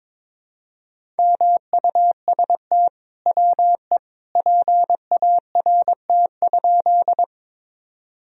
Morse code, heard as a keyed tone, about 23 wpm, 700 Hz. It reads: MUST WE PART?
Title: Sentences from Top 1000 Words - Review Part 1 - 22wpm